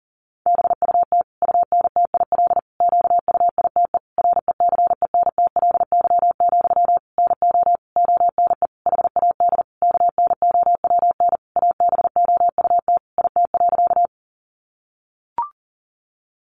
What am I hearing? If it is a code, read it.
Morse code, heard as a keyed tone, about 40 wpm, 700 Hz. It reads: BUT UNTIL QUITE RECENTLY, NO ONE HAD KNOWN ABOUT IT.
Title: Sentences from Top 1200 Words 40wpm